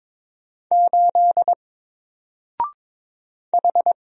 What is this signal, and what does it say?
Morse code, heard as a keyed tone, about 22 wpm, 700 Hz. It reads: 8 H
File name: Single-Letter-Number-22wpm